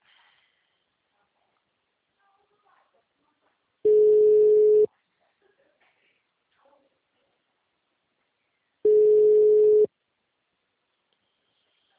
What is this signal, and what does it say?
Ringback tone (Europe, 425 Hz).
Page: Atibaia - SP 1 (CP Cartao)